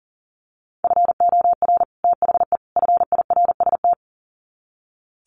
Morse code, heard as a keyed tone, about 40 wpm, 700 Hz. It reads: FOR THE FIRST
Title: Sets of 3 Words 40wpm